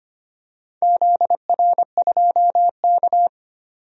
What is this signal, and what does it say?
Morse code, heard as a keyed tone, about 25 wpm, 700 Hz. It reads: ZR2K
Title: DXCC-100 Call Signs - 4 Characters - 25wpm